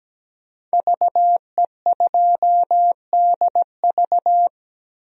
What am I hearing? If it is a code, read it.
Morse code, heard as a keyed tone, about 17 wpm, 700 Hz. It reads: VE2DV
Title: DXCC-100 Call Signs - 5 Characters - Repeated 2x - 17wpm